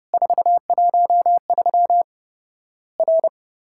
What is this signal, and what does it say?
Morse code, heard as a keyed tone, about 30 wpm, 700 Hz. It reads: 413 R
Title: Worked All Europe DX Contest - QTC Exchange - 30wpm